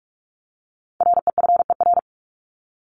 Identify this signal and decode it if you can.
Morse code, heard as a keyed tone, about 44 wpm, 700 Hz. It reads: REFER
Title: Top 751-1000 Words 45wpm